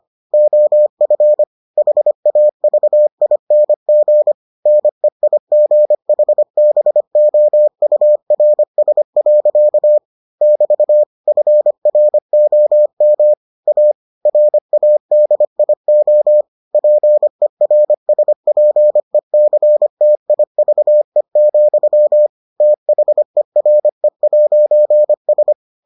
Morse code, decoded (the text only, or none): OF HAVING NEIGHBOURS. <BT> FROM A RADIO PERSPECTIVE, THERE'S